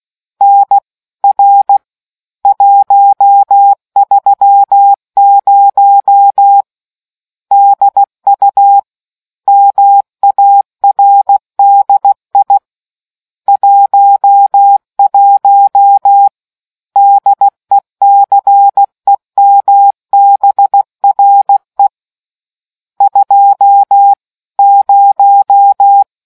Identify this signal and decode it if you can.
Morse code, heard as a keyed tone, about 16 wpm, 800 Hz. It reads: NR 130 DU MARDI 11 DECEMBRE 2 0